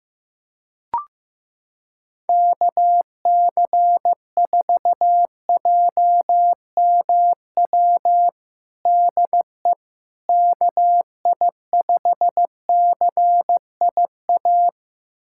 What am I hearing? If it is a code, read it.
Morse code, heard as a keyed tone, about 15 wpm, 700 Hz. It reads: KC4JMW DE KI5CIA
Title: USA Callsign DE Callsign 15wpm